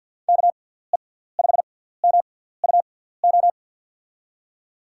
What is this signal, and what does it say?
Morse code, heard as a keyed tone, about 50 wpm, spaced as for 17 wpm, 700 Hz. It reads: KE5MUO
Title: USA Call Signs - 6 Characters - 6x Character Spacing - 50wpm